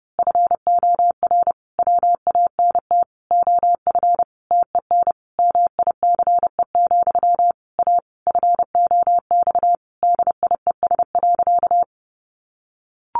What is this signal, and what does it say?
Morse code, heard as a keyed tone, about 30 wpm, 700 Hz. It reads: FOR WANT OF TEN MICE, A FOX DIES.
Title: A Sound Of Thunder 30wpm